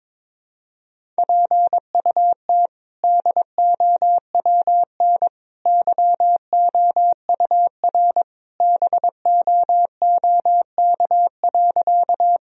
Morse code, heard as a keyed tone, about 22 wpm, 700 Hz. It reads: PUT DOWN YOUR BOOK.
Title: Sentences from Top 1000 Words - Review Part 1 - 22wpm